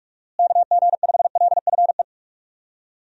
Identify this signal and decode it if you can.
Morse code, heard as a keyed tone, about 46 wpm, 700 Hz. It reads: KG5LFE
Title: USA Call Signs - 6 Characters - Repeated 3x - 45wpm